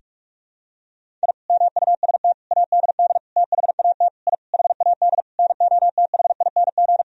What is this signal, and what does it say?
Morse code, heard as a keyed tone, about 46 wpm, 700 Hz. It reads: I MUST ADD THAT I HAD NOTHING
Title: Sentences from Top 1000 Words - Review Part 1 - 45wpm